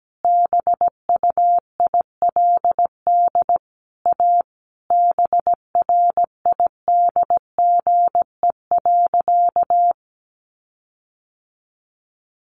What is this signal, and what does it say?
Morse code, heard as a keyed tone, about 17 wpm, 700 Hz. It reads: BUILD A BRIDGE.
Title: Sentences from Top 1400 Words 17wpm